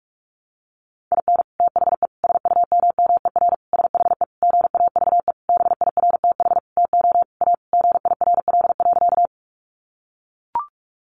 Morse code, heard as a keyed tone, about 44 wpm, 700 Hz. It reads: IN THE SUMMER SHE GAVE BIRTH TO A GIRL.
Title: Sentences from Top 1100 Words 45wpm